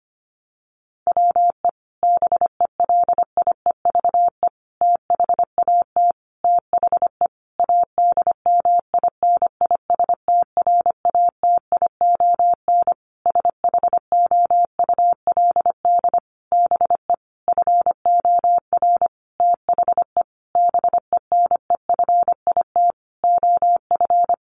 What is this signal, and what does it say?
Morse code, heard as a keyed tone, about 25 wpm, 700 Hz. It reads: WE BELIEVE THAT THE ADMINISTRATION SHOULD BE FOR THE BENEFIT OF
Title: Sentences from Top 1300 Words 25wpm